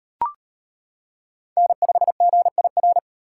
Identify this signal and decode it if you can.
Morse code, heard as a keyed tone, about 38 wpm, 700 Hz. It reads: N5GIR